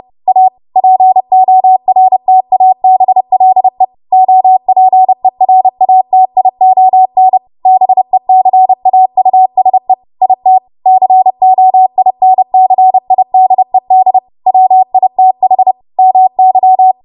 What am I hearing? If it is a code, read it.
Morse code, heard as a keyed tone, about 30 wpm, 750 Hz. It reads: A PORTABLE OPERATION BECAUSE IT COINCIDED WITH MY
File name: Morse / 30WPM